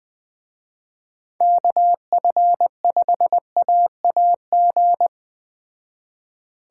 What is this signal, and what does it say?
Morse code, heard as a keyed tone, about 21 wpm, 700 Hz. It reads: KF5AAG
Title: Speed Racing - USA 2x3 Call Signs 17wpm